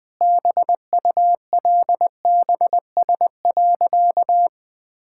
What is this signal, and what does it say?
Morse code, heard as a keyed tone, about 20 wpm, 700 Hz. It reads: BULBS.
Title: Easy Sentences with Hints - 20wpm